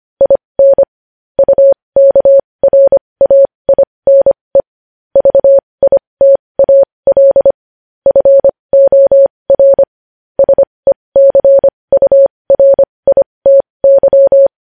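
Morse code, heard as a keyed tone, about 25 wpm, spaced as for 20 wpm, 550 Hz. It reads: IN UKRAINE VITAL FOR SECURITY
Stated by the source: Morse code news headlines sent at 20 wpm.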